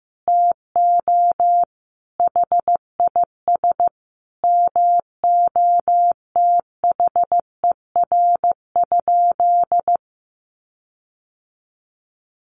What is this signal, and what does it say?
Morse code, heard as a keyed tone, about 15 wpm, 700 Hz. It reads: TO HIS MOTHER?
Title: Sentences from Top 1300 Words 15wpm